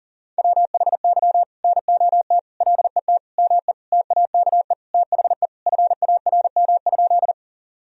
Morse code, heard as a keyed tone, about 40 wpm, 700 Hz. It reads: WHY NOT LET ME TAKE THE FARM?
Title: Sentences from Top 1000 Words - Review Part 2 - 40wpm